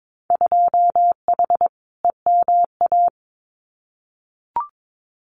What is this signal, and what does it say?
Morse code, heard as a keyed tone, about 22 wpm, 700 Hz. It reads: 2H EMA
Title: Winter Field Day Exchanges 22wpm